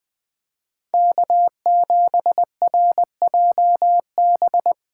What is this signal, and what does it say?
Morse code, heard as a keyed tone, about 21 wpm, 700 Hz. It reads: K7RJB